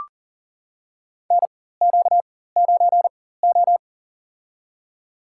Morse code, heard as a keyed tone, about 40 wpm, spaced as for 19 wpm, 700 Hz. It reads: NQ9O
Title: USA Call Signs - 4 Characters - 4x Character Spacing - 40wpm